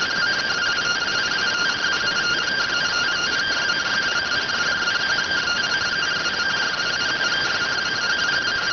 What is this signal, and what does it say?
Two-tone teleprinter signal, about 170 Hz shift.